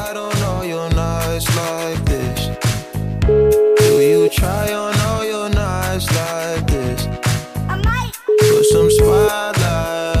Ringback tone (Europe, 425 Hz), with noise.